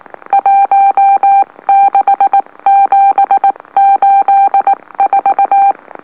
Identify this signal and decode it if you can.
Morse code, heard as a keyed tone, about 19 wpm, spaced as for 17 wpm, 790 Hz. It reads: 16784
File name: M04- -2345utc-LOLO